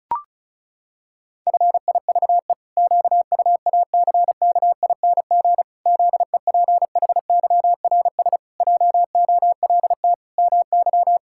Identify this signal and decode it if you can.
Morse code, heard as a keyed tone, about 36 wpm, 700 Hz. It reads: FIVE QUACKING ZEPHYRS JOLT MY
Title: Pangram Challenge v2 - 35wpm